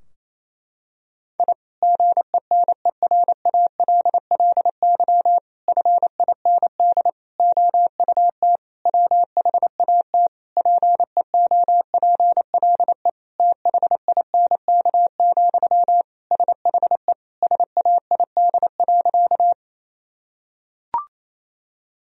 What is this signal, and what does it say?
Morse code, heard as a keyed tone, about 28 wpm, 700 Hz. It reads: I GENERALLY FIND OUT WHAT PEOPLE THINK, SHE SAID.
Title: Sentences from Top 1000 Words - Review Part 4 - 28wpm